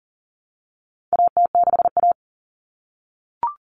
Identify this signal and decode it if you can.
Morse code, heard as a keyed tone, about 40 wpm, 700 Hz. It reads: AT6A
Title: Common Cut Numbers - 4-Digit 40wpm